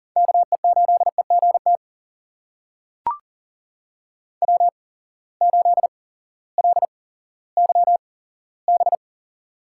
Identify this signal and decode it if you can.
Morse code, heard as a keyed tone, about 39 wpm, spaced as for 18 wpm, 700 Hz. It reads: KE8EGT W8LYB
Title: USA Call Signs - 8x Character Spacing - 40wpm